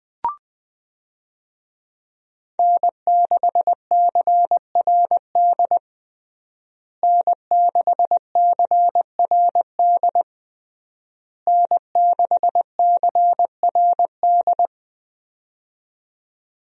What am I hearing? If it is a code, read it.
Morse code, heard as a keyed tone, about 20 wpm, 700 Hz. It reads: N6CRD N6CRD N6CRD
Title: USA Call Signs - 5 Characters - Repeated 3x - 20wpm